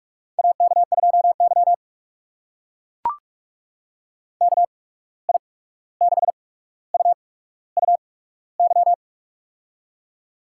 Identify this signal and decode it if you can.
Morse code, heard as a keyed tone, about 44 wpm, spaced as for 17 wpm, 700 Hz. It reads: AK2Y KI6UUY
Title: USA Call Signs - 8x Character Spacing - 45wpm